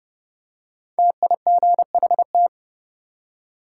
Morse code, heard as a keyed tone, about 31 wpm, 700 Hz. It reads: TIGHT
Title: Five Letter Words - 30wpm